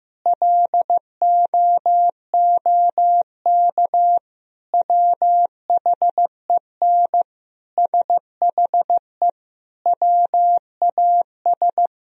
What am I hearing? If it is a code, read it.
Morse code, heard as a keyed tone, about 15 wpm, 700 Hz. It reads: LOOK WHEN SHE WAS
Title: Sentences from Top 1000 Words - Review Part 3 - 15wpm